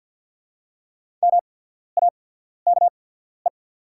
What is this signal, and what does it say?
Morse code, heard as a keyed tone, about 48 wpm, spaced as for 14 wpm, 690 Hz. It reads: MAKE
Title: 8x Character Spacing - Top 300 Words - 50wpm